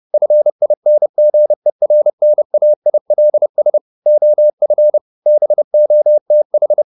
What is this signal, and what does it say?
Morse code, heard as a keyed tone, about 30 wpm, 600 Hz. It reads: FINGERNAILS OF BOTH
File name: the old man and the sea 2 30wpm